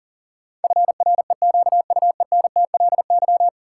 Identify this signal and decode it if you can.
Morse code, heard as a keyed tone, about 40 wpm, 700 Hz. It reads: FREQUENTLY
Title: Sentences from Top 1300 Words - New Word Drill Alt - 40wpm